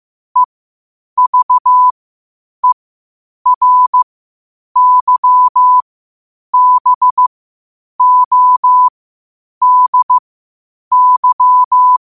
Morse code, heard as a keyed tone, about 15 wpm, spaced as for 8 wpm, 1000 Hz. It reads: EVERYBODY
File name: qotdfiles / 8wpm